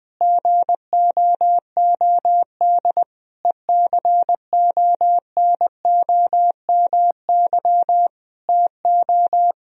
Morse code, heard as a keyed tone, about 20 wpm, 700 Hz. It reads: GOOD ECONOMY TO
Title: Sentences from Top 1000 Words - Review Part 3 - 20wpm